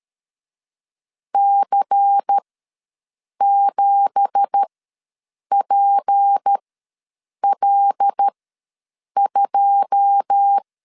Morse code, heard as a keyed tone, about 13 wpm, spaced as for 7 wpm, 800 Hz. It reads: C7PL2
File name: LEZIONI / 9 2867 ws 12wpm 3431